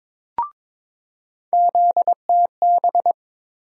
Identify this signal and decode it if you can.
Morse code, heard as a keyed tone, about 22 wpm, 700 Hz. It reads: ZTB
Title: USA Call Sign Postfixes - 3 Letters 22wpm